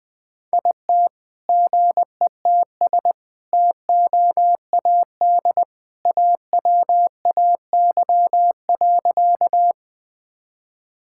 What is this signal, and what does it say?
Morse code, heard as a keyed tone, about 20 wpm, 700 Hz. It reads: IT GETS TOAD AWAY.
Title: All-Age Appropriate Jokes - 20wpm